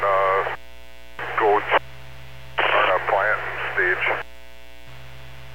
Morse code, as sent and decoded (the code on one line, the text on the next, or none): none